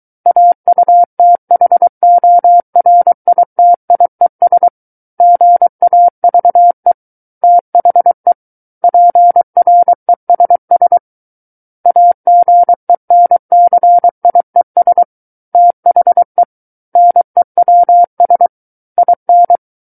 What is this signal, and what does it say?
Morse code, heard as a keyed tone, about 23 wpm, 700 Hz. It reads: AUTHORITIES GAVE THE PRESS AGENCIES THE NEWS IN
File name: Book One / 14-Chapter 14 - 23 WPM